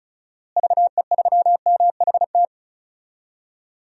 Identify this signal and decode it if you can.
Morse code, heard as a keyed tone, about 34 wpm, 700 Hz. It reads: VE3MHT
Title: DXCC-100 Call Signs - 6 Characters - Repeated 3x - 35wpm